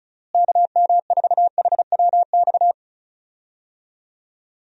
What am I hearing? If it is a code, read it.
Morse code, heard as a keyed tone, about 34 wpm, 700 Hz. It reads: KM4HWX
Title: USA Call Signs - 6 Characters - Repeated 2x - 35wpm